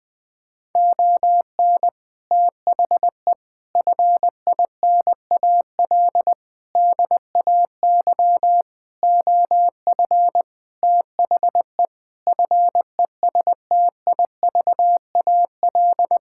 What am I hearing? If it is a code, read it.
Morse code, heard as a keyed tone, about 20 wpm, 700 Hz. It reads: ON THE FINAL DAY OF THE FESTIVAL
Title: Sentences with 6-Word Phrases - 20wpm